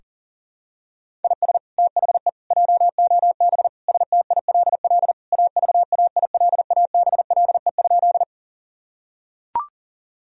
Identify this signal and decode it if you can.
Morse code, heard as a keyed tone, about 40 wpm, 700 Hz. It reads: IS THE JOB STILL AVAILABLE?
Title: Sentences from Top 1000 Words - Review Part 3 - 40wpm